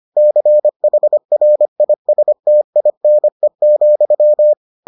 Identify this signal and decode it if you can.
Morse code, heard as a keyed tone, about 25 wpm, 600 Hz. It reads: CHRISTINE,